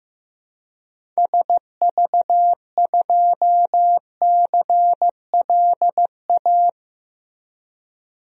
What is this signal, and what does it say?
Morse code, heard as a keyed tone, about 16 wpm, 700 Hz. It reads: SV2CLA
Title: DXCC-100 Call Signs - 6 Characters - Repeated 2x - 15wpm